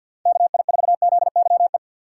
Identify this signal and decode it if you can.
Morse code, heard as a keyed tone, about 47 wpm, 700 Hz. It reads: KI4ZYE
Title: USA Call Signs - 6 Characters - 3x Character Spacing - 50wpm